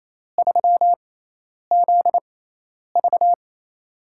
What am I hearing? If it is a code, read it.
Morse code, heard as a keyed tone, about 27 wpm, spaced as for 10 wpm, 700 Hz. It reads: 3ZV
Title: USA Call Signs - 4 Characters - 6x Character Spacing - 28wpm